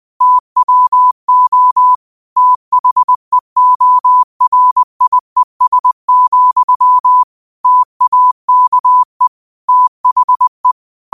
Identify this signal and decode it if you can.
Morse code, heard as a keyed tone, about 20 wpm, 1000 Hz. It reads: TWO THEORIES, TAKE THE
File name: qotdfiles / 20wpm